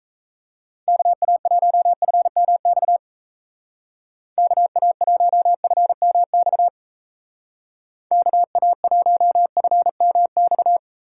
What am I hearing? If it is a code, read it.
Morse code, heard as a keyed tone, about 35 wpm, 700 Hz. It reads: KA1FMX KA1FMX KA1FMX